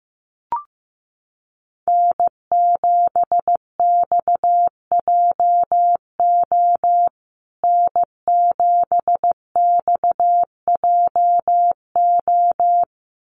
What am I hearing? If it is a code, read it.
Morse code, heard as a keyed tone, about 15 wpm, 700 Hz. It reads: N7XJO N7XJO
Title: USA Call Signs Repeated 2x 15wpm